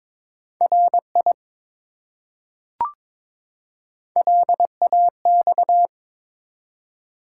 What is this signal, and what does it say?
Morse code, heard as a keyed tone, about 22 wpm, 700 Hz. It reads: RI LAX
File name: ARRL Sections v2 - 22wpm